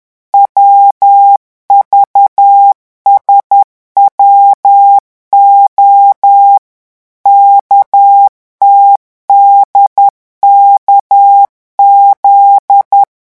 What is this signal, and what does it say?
Morse code, heard as a keyed tone, about 10 wpm, 800 Hz. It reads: WVSWO KTDKZ
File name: morse mp3 / revision11 lettres vitesse 10